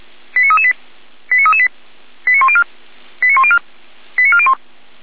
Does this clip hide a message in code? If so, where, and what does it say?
Yes: five-tone selective call (ZVEI) from 0.4 to 4.5 s, heard as quick runs of five tones, tones of about 70 ms. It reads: 89338 89338 89104 89104 89401